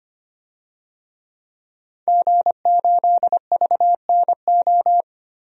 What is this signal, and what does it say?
Morse code, heard as a keyed tone, about 25 wpm, 700 Hz. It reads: G8VNO
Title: DXCC-100 Call Signs - 5 Characters - 25wpm